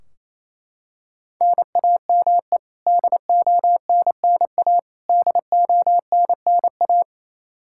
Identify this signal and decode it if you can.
Morse code, heard as a keyed tone, about 28 wpm, 700 Hz. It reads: NAME DONNA DONNA
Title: K1USN SST QSO Practice 28wpm